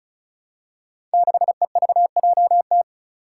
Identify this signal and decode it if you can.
Morse code, heard as a keyed tone, about 36 wpm, 700 Hz. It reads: 6EVJT
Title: Five Challenging Letter-Number Combinations 35wpm